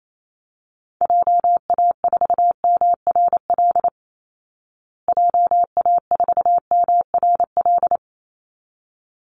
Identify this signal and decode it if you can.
Morse code, heard as a keyed tone, about 28 wpm, 700 Hz. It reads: JA4MRL JA4MRL
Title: Call Signs Repeated 3x - 28wpm